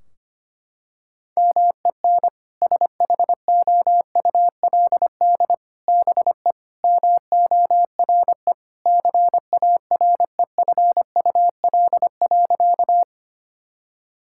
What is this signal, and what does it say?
Morse code, heard as a keyed tone, about 25 wpm, 700 Hz. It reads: MEN SHOULD BE MORE CAREFUL.
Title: Sentences from Top 1600 Words 25wpm